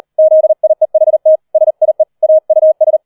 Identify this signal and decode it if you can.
Morse code, heard as a keyed tone, about 38 wpm, 630 Hz. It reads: ZIEHT SIE AUS